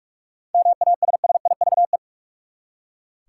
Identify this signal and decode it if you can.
Morse code, heard as a keyed tone, about 43 wpm, 700 Hz. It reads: MASSIVE
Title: Seven Letter Words - 45wpm